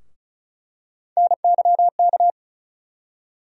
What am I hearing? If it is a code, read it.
Morse code, heard as a keyed tone, about 34 wpm, 700 Hz. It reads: NYK
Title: USA Call Sign Postfixes - 3 Letters 35wpm